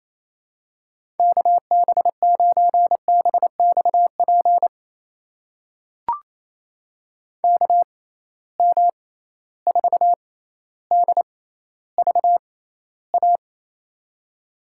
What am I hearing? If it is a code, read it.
Morse code, heard as a keyed tone, about 28 wpm, spaced as for 14 wpm, 700 Hz. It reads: KB9BXP KM4DVA
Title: USA Call Signs - 6 Characters - 6x Character Spacing - 28wpm